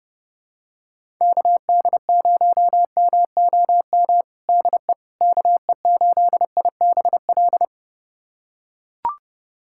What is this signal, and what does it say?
Morse code, heard as a keyed tone, about 30 wpm, 700 Hz. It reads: KD0MOM DE KE8IBL
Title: USA Callsign DE Callsign - 6 Characters - 30wpm